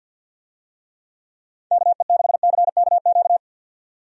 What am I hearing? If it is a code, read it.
Morse code, heard as a keyed tone, about 51 wpm, 700 Hz. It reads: KE6CKQ
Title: USA Call Signs - 6 Characters - 50wpm